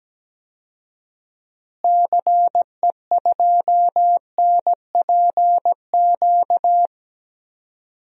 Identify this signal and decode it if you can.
Morse code, heard as a keyed tone, about 17 wpm, 700 Hz. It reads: CE2NPQ
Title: DXCC-100 Call Signs - 6 Characters - Repeated 3x - 17wpm